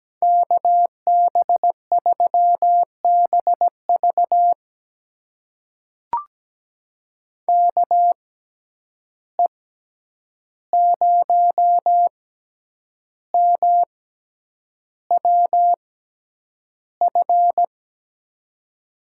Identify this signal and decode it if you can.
Morse code, heard as a keyed tone, about 17 wpm, spaced as for 8 wpm, 700 Hz. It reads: KB3BV KE0MWF
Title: USA Call Signs - 6x Character Spacing - 17wpm